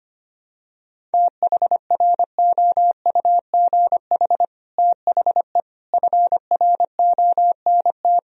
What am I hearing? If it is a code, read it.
Morse code, heard as a keyed tone, about 25 wpm, 700 Hz. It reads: THROUGH THE FRONT
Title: Sentences from Top 1000 Words - Review Part 2 - 25wpm